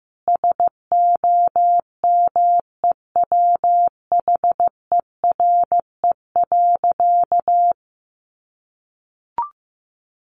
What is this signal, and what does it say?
Morse code, heard as a keyed tone, about 15 wpm, 700 Hz. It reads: SOMEWHERE.
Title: Sentences from Top 1400 Words 15wpm